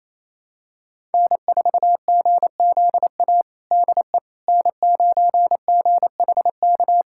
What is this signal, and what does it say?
Morse code, heard as a keyed tone, about 28 wpm, 700 Hz. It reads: N4GZA DE N9GHK
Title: USA Callsign DE Callsign - 5 Characters - 28wpm